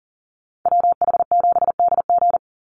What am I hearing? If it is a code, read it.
Morse code, heard as a keyed tone, about 40 wpm, 700 Hz. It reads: WH7DG
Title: USA Call Signs 40wpm